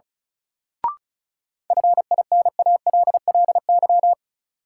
Morse code, heard as a keyed tone, about 36 wpm, 700 Hz. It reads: FINALLY
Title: Sentences from Top 1000 Words - New Word Drill - 35wpm